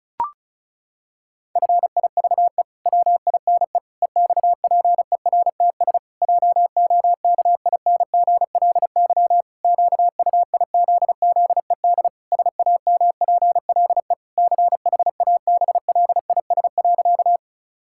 Morse code, heard as a keyed tone, about 36 wpm, 700 Hz. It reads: FIVE WINE EXPERTS JOKINGLY QUIZZED SAMPLE CHABLIS.
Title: Pangram Challenge v2 - 35wpm